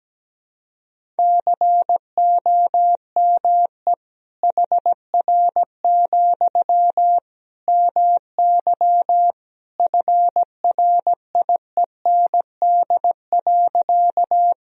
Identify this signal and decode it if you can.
Morse code, heard as a keyed tone, about 17 wpm, 700 Hz. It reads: COME HR, MY FRIEND.
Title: Sentences from Top 300 Words and Abbreviations 17wpm